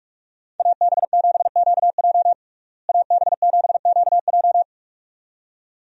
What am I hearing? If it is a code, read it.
Morse code, heard as a keyed tone, about 46 wpm, 700 Hz. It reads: AB7QJ AB7QJ
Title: USA Call Signs - 5 Characters - Repeated 2x - 45wpm